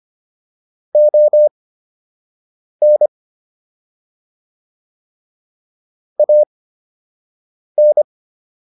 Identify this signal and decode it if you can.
Morse code, heard as a keyed tone, about 25 wpm, spaced as for 6 wpm, 600 Hz. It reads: ON AN